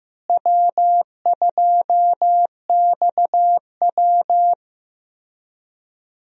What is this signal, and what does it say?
Morse code, heard as a keyed tone, about 14 wpm, 700 Hz. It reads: W2XW
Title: USA Call Signs - 4 Characters - Repeated 3x - 15wpm